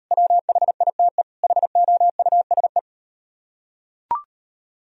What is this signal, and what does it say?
Morse code, heard as a keyed tone, about 38 wpm, 700 Hz. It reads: WHITE HOUSE